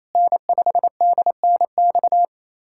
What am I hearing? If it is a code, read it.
Morse code, heard as a keyed tone, about 28 wpm, 700 Hz. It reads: N5DNX
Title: USA Callsign DE Callsign - 5 Characters - 28wpm